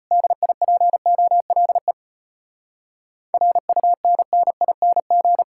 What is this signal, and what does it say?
Morse code, heard as a keyed tone, about 33 wpm, 700 Hz. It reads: DIPOLE RUNNING